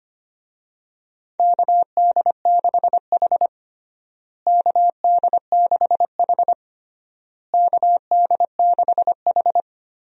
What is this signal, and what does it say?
Morse code, heard as a keyed tone, about 25 wpm, 700 Hz. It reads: KD6H KD6H KD6H
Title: USA Call Signs - 4 Characters - Repeated 3x - 25wpm